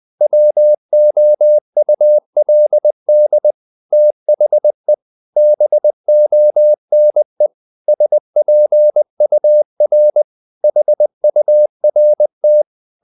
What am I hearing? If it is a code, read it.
Morse code, heard as a keyed tone, about 20 wpm, 600 Hz. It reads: WOULD THE BONE SPUR HURT
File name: the old man and the sea / the old man and the sea part 1 20wpm